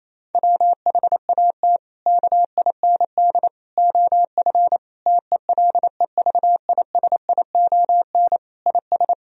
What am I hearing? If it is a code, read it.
Morse code, heard as a keyed tone, about 28 wpm, 700 Hz. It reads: WHAT KIND OF TELEVISION IS
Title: Sentences from Top 1000 Words - Review Part 4 - 28wpm